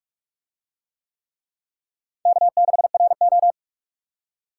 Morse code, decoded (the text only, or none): K6RO